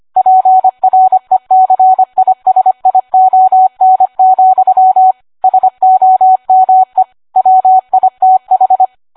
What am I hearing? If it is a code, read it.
Morse code, heard as a keyed tone, about 25 wpm, 750 Hz. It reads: PRECISION, SOME WITH